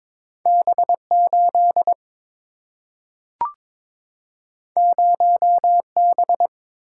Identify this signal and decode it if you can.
Morse code, heard as a keyed tone, about 22 wpm, 700 Hz. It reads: B8 0B
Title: Single-Letter Single-Number 22wpm